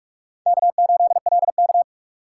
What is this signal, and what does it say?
Morse code, heard as a keyed tone, about 43 wpm, 700 Hz. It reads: K8LK
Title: USA Call Signs - 4 Characters - 2x Character Spacing - 45wpm